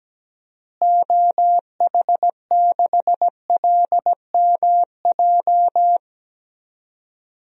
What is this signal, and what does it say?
Morse code, heard as a keyed tone, about 17 wpm, 700 Hz. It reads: OH6LMJ
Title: DXCC-100 Call Signs - 6 Characters - Repeated 3x - 17wpm